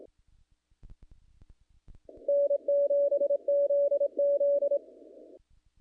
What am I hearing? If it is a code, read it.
Morse code, heard as a keyed tone, about 26 wpm, 570 Hz. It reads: N7ZZ
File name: contests / naqp-cw-k5tr-op-k5pi- -1900